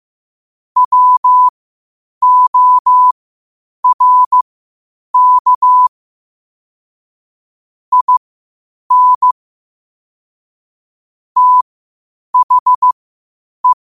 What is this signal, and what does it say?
Morse code, heard as a keyed tone, about 15 wpm, spaced as for 8 wpm, 1000 Hz. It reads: WORK IN THE